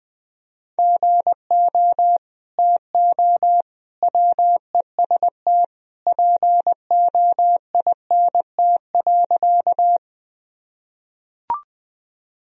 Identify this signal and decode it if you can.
Morse code, heard as a keyed tone, about 21 wpm, 700 Hz. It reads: GO TO WEST POINT.
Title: Sentences from Top 1000 Words - Review Part 2 - 20wpm